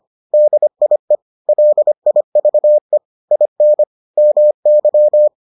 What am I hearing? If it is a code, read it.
Morse code, heard as a keyed tone, about 25 wpm, 600 Hz. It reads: DIE LIVE IN MY